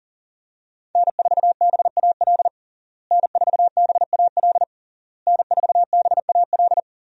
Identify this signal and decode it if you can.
Morse code, heard as a keyed tone, about 37 wpm, 700 Hz. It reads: N4BAL N4BAL N4BAL
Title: USA Call Signs - 5 Characters - Repeated 3x - 40wpm